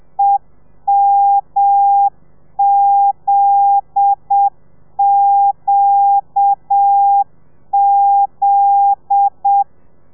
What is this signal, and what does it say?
Morse code, heard as a keyed tone, about 7 wpm, 800 Hz. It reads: EMZQZ